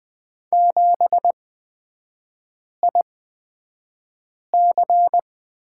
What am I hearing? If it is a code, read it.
Morse code, heard as a keyed tone, about 20 wpm, spaced as for 5 wpm, 690 Hz. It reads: 7IC
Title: 236 Lesson 38e - Letter q - Three Characters 20wpm